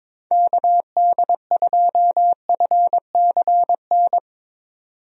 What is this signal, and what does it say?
Morse code, heard as a keyed tone, about 22 wpm, 700 Hz. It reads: KD2FCN
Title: USA Call Signs - 6 Characters - 22wpm